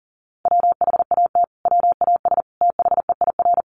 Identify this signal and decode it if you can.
Morse code, heard as a keyed tone, about 40 wpm, 700 Hz. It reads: WHAT WAS THEIR
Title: Sentences from Top 1400 Words 40wpm